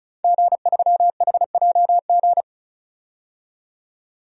Morse code, heard as a keyed tone, about 36 wpm, 700 Hz. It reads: G3HJG
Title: DXCC-100 Call Signs - 5 Characters - 35wpm